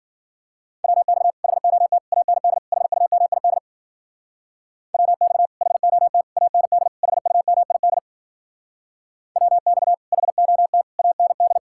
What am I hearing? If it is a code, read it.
Morse code, heard as a keyed tone, about 52 wpm, 700 Hz. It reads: WX HOT AND HUMID WX HOT AND HUMID WX HOT AND